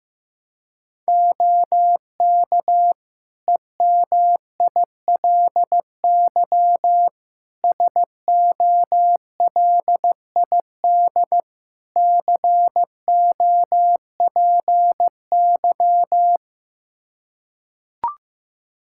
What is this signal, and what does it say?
Morse code, heard as a keyed tone, about 15 wpm, 700 Hz. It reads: OK EMILY SOLID COPY
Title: Practice QSO - Exchange 3 - More Info - 15wpm